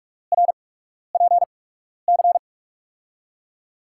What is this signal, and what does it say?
Morse code, heard as a keyed tone, about 44 wpm, spaced as for 12 wpm, 700 Hz. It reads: RPC
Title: USA Call Signs - 5 Characters - 8x Character Spacing - 45wpm